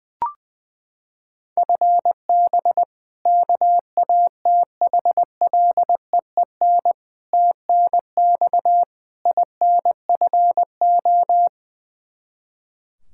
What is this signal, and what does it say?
Morse code, heard as a keyed tone, about 20 wpm, 700 Hz. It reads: FB KATHLEEN TNX INFO
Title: Practice QSO - Exchange 3 - More Info - 20wpm